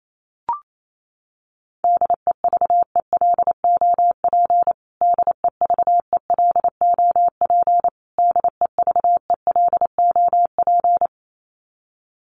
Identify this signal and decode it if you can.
Morse code, heard as a keyed tone, about 28 wpm, 700 Hz. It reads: DEVELOP DEVELOP DEVELOP
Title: Top 500 Words Repeated 28wpm